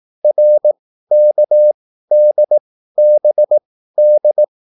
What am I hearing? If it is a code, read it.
Morse code, heard as a keyed tone, about 18 wpm, spaced as for 13 wpm, 600 Hz. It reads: RKDBD
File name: QSL / CW lesson